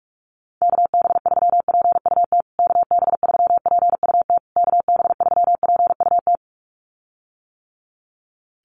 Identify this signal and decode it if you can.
Morse code, heard as a keyed tone, about 46 wpm, 700 Hz. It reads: KB3PUT KB3PUT KB3PUT
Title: USA Call Signs Repeated 3x 45wpm